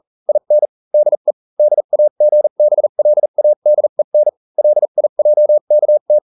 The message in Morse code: IN DE DAGBLADEN LIJKT